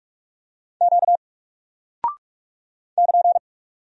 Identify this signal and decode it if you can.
Morse code, heard as a keyed tone, about 44 wpm, 700 Hz. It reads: Q <KN>
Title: Single Character (Letter:Number:Punctuation:Prosign) 45wpm